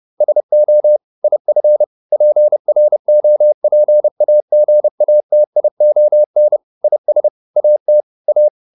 Morse code, heard as a keyed tone, about 30 wpm, 600 Hz. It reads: SO IF PROPAGATION IS AT A